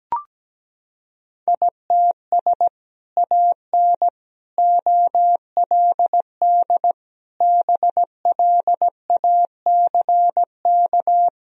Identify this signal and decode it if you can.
Morse code, heard as a keyed tone, about 17 wpm, 700 Hz. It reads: ITS AN OLD BLACK
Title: Sentences from Top 1000 Words - Review Part 2 - 17wpm